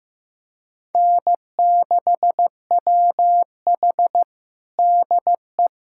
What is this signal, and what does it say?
Morse code, heard as a keyed tone, about 15 wpm, 700 Hz. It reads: N6WH DE
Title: USA Callsign DE Callsign - 4 Characters - 15wpm